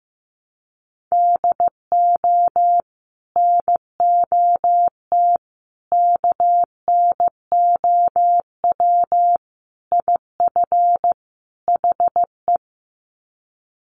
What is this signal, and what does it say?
Morse code, heard as a keyed tone, about 15 wpm, 700 Hz. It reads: DO NOT KNOW IF HE
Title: Sets of 5 Words Encore2 15wpm